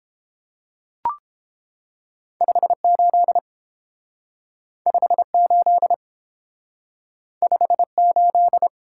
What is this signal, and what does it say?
Morse code, heard as a keyed tone, about 30 wpm, 700 Hz. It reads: 58 58 58